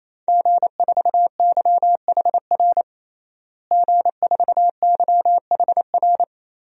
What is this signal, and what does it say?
Morse code, heard as a keyed tone, about 28 wpm, 700 Hz. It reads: G4YHR G4YHR
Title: DXCC-100 Call Signs - 5 Characters - Repeated 3x - 28wpm